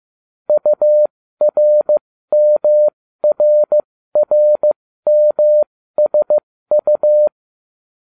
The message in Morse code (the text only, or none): URMRRMSU